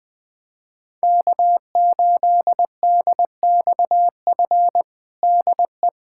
Morse code, heard as a keyed tone, about 20 wpm, 700 Hz. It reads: K8DXF DE
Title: USA Callsign DE Callsign - 5 Characters - 20wpm